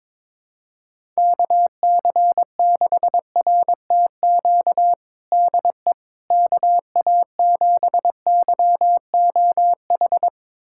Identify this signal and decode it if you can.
Morse code, heard as a keyed tone, about 22 wpm, 700 Hz. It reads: KC6RTQ DE KA7YOH
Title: USA Callsign DE Callsign - 6 Characters - 22wpm